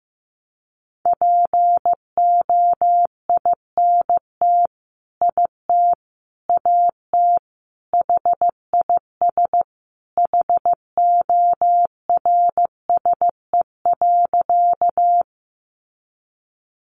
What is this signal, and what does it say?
Morse code, heard as a keyed tone, about 14 wpm, 700 Hz. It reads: POINT IT AT HIS HORSE.
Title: Sentences from Top 300 Words 15wpm